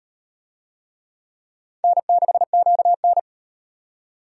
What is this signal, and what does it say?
Morse code, heard as a keyed tone, about 38 wpm, 700 Hz. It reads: N6QN